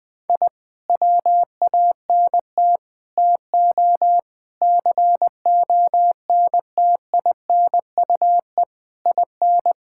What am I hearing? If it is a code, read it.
Morse code, heard as a keyed tone, about 20 wpm, 700 Hz. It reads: I WANT TO CONTINUE IN
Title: Sentences from Top 1000 Words - Review Part 6 - 20wpm